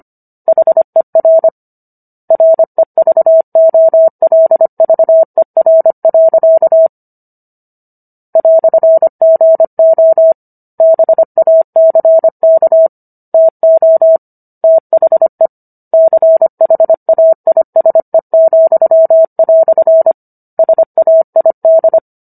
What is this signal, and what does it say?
Morse code, heard as a keyed tone, about 25 wpm, 650 Hz. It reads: HER REVOLVER. "GO BACK TO THE CHAISE," SAID